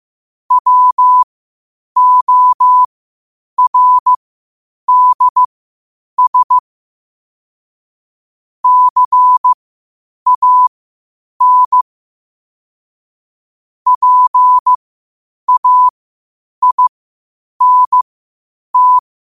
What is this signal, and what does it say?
Morse code, heard as a keyed tone, about 15 wpm, spaced as for 8 wpm, 1000 Hz. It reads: WORDS CAN PAINT
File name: qotdfiles / 8wpm